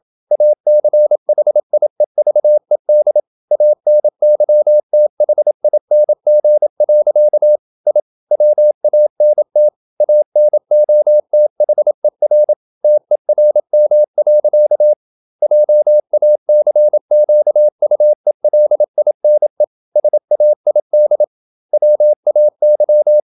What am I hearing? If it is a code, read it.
Morse code, heard as a keyed tone, about 27 wpm, 600 Hz. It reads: ACHIEVED ANYTHING. I WANT ANOTHER TERM. JACQUELINE SAID WAY
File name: Never 0027wpm